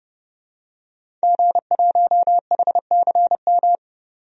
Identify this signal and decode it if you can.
Morse code, heard as a keyed tone, about 30 wpm, 700 Hz. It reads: G1HCM
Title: DXCC-100 Call Signs - 5 Characters - 30wpm